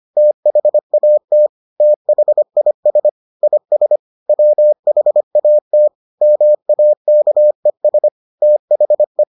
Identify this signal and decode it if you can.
Morse code, heard as a keyed tone, about 25 wpm, 600 Hz. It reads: THAT THIS IS WHAT MAKES THE